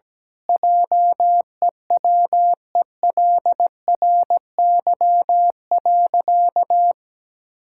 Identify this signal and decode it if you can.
Morse code, heard as a keyed tone, about 17 wpm, 700 Hz. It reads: JEWELRY.
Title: Pangram Challenge v2 - 17wpm